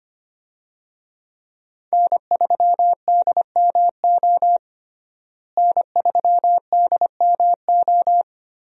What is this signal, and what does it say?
Morse code, heard as a keyed tone, about 25 wpm, 700 Hz. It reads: N3DMO N3DMO
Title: USA Call Signs - 5 Characters - Repeated 3x - 25wpm